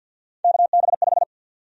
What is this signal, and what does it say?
Morse code, heard as a keyed tone, about 53 wpm, 700 Hz. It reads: KB5
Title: USA Call Sign Prefixes - 50wpm